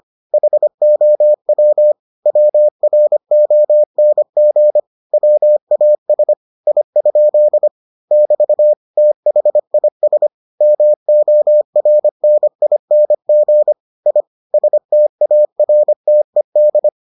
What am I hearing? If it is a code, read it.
Morse code, heard as a keyed tone, about 25 wpm, 600 Hz. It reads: HOW WRONG WAS I? <BT> THIS MORNING I STARTED